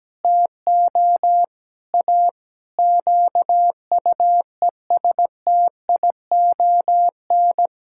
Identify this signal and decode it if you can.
Morse code, heard as a keyed tone, about 17 wpm, 700 Hz. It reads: TO A QUESTION
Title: Sentences from Top 1000 Words - Review Part 6 - 17wpm